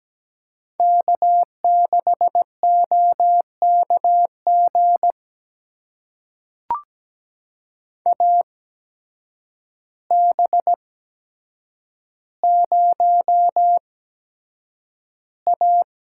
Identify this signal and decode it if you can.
Morse code, heard as a keyed tone, about 17 wpm, spaced as for 8 wpm, 700 Hz. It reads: K6OKG AB0A
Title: USA Call Signs - 5 Characters - 8x Character Spacing - 17wpm